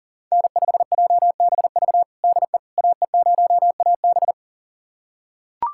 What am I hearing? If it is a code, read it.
Morse code, heard as a keyed tone, about 40 wpm, 700 Hz. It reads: N5JBV DE AE0AB
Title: USA Callsign DE Callsign - 5 Characters - 40wpm